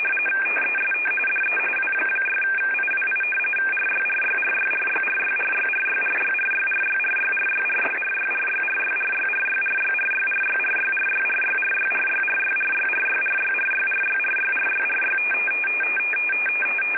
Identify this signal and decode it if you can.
RTTY, about 45.45 baud. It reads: TEST DE W6IWI RYRYRYRYRYRYRYRYRYRYRYRYRYRYRYRYRYRYRYRYRYRYRYRYRYRYRYRYRYRYRYRYRYRYRY
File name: rtty / websdr recording 14089.1kHz